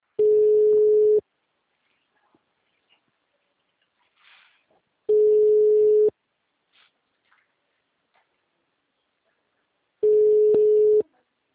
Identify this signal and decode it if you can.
Ringback tone (Europe, 425 Hz).